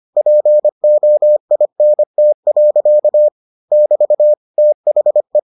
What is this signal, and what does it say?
Morse code, heard as a keyed tone, about 25 wpm, 600 Hz. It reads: POINT. <BT> THE